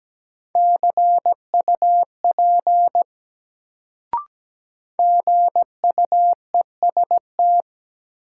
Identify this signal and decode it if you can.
Morse code, heard as a keyed tone, about 17 wpm, 700 Hz. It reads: CUP GUEST
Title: Sentences from Top 1600 Words - New Word Drill - 17wpm